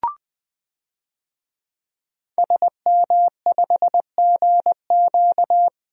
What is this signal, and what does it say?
Morse code, heard as a keyed tone, about 20 wpm, 700 Hz. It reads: SM5GQ
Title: DXCC-100 Call Signs - 5 Characters - Repeated 3x - 20wpm